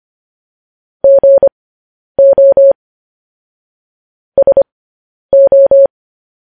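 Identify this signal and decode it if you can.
Morse code, heard as a keyed tone, about 25 wpm, spaced as for 10 wpm, 550 Hz. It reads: GO SO
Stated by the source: Morse code transcription of news headlines sent at 10 wpm.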